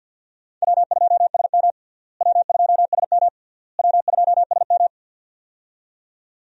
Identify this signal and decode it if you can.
Morse code, heard as a keyed tone, about 52 wpm, 700 Hz. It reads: W2SM W2SM W2SM
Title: USA Call Signs - 4 Characters - Repeated 3x - 50wpm